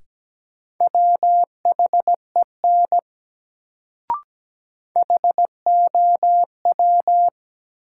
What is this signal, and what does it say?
Morse code, heard as a keyed tone, about 17 wpm, 700 Hz. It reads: WHEN HOW
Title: Sentences from Top 100 Words - New Word Drill - 17wpm